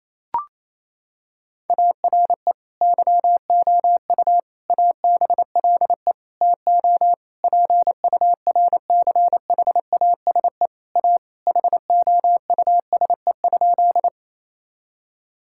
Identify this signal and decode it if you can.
Morse code, heard as a keyed tone, about 27 wpm, 700 Hz. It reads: ARE YOU ABLE TO PURCHASE A HOUSE?
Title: Sentences from Top 1000 Words - Review Part 5 - 28wpm